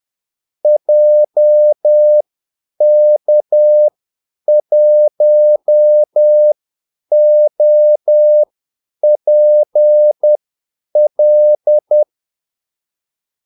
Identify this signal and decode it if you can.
Morse code, heard as a keyed tone, about 10 wpm, spaced as for 8 wpm, 600 Hz. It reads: JK1OPL
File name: Callsigns-10-8wpm